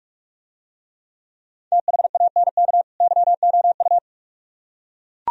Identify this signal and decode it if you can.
Morse code, heard as a keyed tone, about 41 wpm, 700 Hz. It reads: THANK YOU